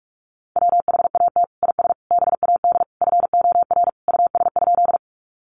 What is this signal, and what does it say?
Morse code, heard as a keyed tone, about 44 wpm, 700 Hz. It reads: WHAT IS BAD FOR US?
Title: Sentences from Top 700 Words 45wpm